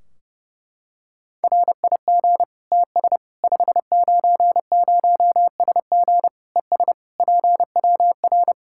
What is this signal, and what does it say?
Morse code, heard as a keyed tone, about 32 wpm, 700 Hz. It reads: RIG TS 590SG ES PWR
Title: K1USN SST QSO Practice 30wpm